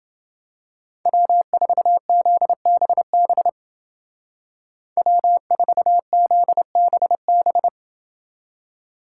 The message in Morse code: W4ZBB W4ZBB

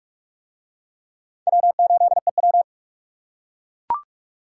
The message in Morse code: W8EW

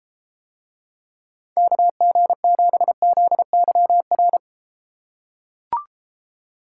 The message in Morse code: KG7ZYR

